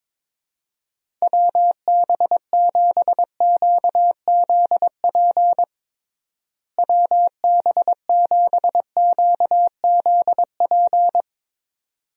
Morse code, heard as a keyed tone, about 22 wpm, 700 Hz. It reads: WB7QZP WB7QZP
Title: USA Call Signs - 6 Characters - Repeated 2x - 22wpm